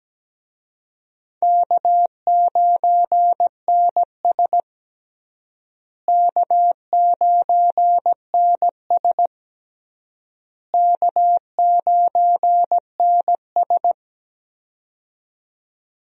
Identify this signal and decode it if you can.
Morse code, heard as a keyed tone, about 17 wpm, 700 Hz. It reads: K9NS K9NS K9NS
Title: USA Call Signs - 4 Characters - Repeated 3x - 17wpm